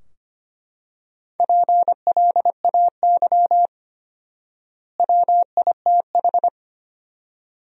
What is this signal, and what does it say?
Morse code, heard as a keyed tone, about 25 wpm, 700 Hz. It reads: PLAY WITH
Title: 4x Word-Spacing - Sentences from Top 200 Words - 25wpm